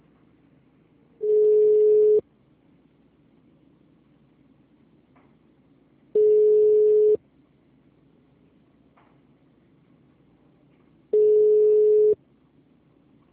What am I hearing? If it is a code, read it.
Ringback tone (Europe, 425 Hz).